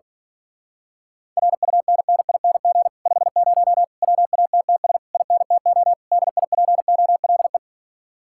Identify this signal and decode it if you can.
Morse code, heard as a keyed tone, about 43 wpm, 700 Hz. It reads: RUNNING 50 WATTS INTO DIPOLE
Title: Speed-Racing Common QSO Phrases - 35wpm